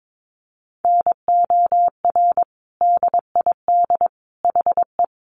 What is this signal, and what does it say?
Morse code, heard as a keyed tone, about 22 wpm, 700 Hz. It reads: NOR DID HE
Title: Sentences from Top 1500 Words 22wpm